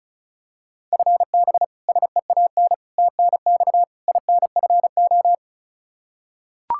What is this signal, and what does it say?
Morse code, heard as a keyed tone, about 36 wpm, 700 Hz. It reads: FB SEAN TNX INFO
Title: Practice QSO - Exchange 3 - More Info - 35wpm